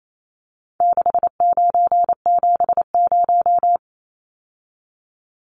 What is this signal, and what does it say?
Morse code, heard as a keyed tone, about 28 wpm, 700 Hz. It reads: 6970
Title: Numbers - 4-Digit 28wpm